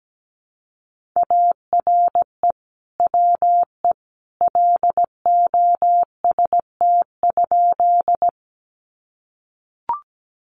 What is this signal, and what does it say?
Morse code, heard as a keyed tone, about 17 wpm, 700 Hz. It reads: ARE WE LOST?
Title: Sentences from Top 1400 Words 17wpm